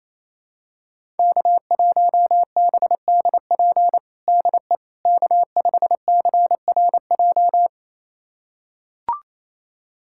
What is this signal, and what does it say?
Morse code, heard as a keyed tone, about 28 wpm, 700 Hz. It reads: K1BDP DE K5CRJ
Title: USA Callsign DE Callsign - 5 Characters - 28wpm